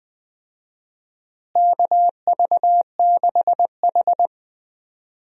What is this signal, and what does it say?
Morse code, heard as a keyed tone, about 21 wpm, 700 Hz. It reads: KV6H
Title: USA Call Signs - 4 Characters - Repeated 3x - 20wpm